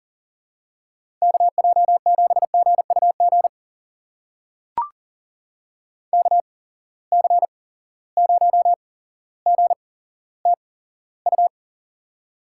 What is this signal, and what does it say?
Morse code, heard as a keyed tone, about 42 wpm, spaced as for 17 wpm, 700 Hz. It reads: KJ7GUG KC0GTU
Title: USA Call Signs - 6 Characters - 8x Character Spacing - 40wpm